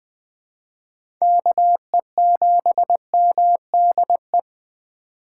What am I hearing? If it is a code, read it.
Morse code, heard as a keyed tone, about 20 wpm, 700 Hz. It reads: KE7MDE
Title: USA Call Signs - 6 Characters - 3x Character Spacing - 20wpm